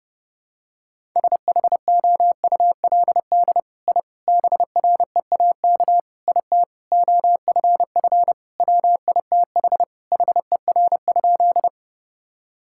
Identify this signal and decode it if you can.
Morse code, heard as a keyed tone, about 30 wpm, 700 Hz. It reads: SHOULD I BREAK IT OFF WITH HER?
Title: Sentences from Top 1000 Words - Review Part 6 - 30wpm